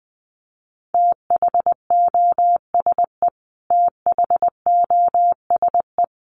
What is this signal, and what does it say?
Morse code, heard as a keyed tone, about 20 wpm, 700 Hz. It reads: THOSE THOSE
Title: Top 500 Words Repeated 20wpm